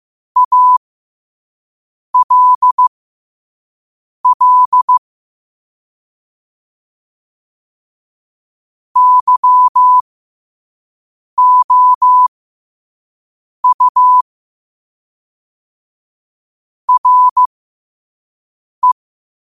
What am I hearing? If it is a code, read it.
Morse code, heard as a keyed tone, about 15 wpm, spaced as for 5 wpm, 1000 Hz. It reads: ALL YOU RE
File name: 5wpm